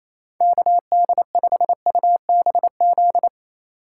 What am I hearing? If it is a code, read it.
Morse code, heard as a keyed tone, about 28 wpm, 700 Hz. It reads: KD5UBZ
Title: USA Call Signs - 6 Characters - 3x Character Spacing - 28wpm